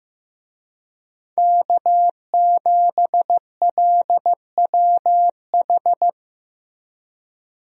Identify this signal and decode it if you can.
Morse code, heard as a keyed tone, about 16 wpm, 700 Hz. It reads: K7LWH
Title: USA Call Signs - 5 Characters - Repeated 3x - 15wpm